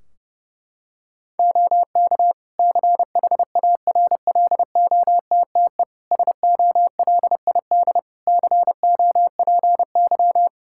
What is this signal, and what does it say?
Morse code, heard as a keyed tone, about 30 wpm, 700 Hz. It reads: OK CHARLOTTE SOLID COPY
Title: Practice QSO - Exchange 3 - More Info - 30wpm